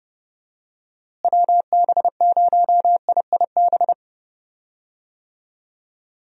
Morse code, heard as a keyed tone, about 30 wpm, 700 Hz. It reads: WB0IIB
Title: USA Call Signs - 6 Characters - Repeated 3x - 30wpm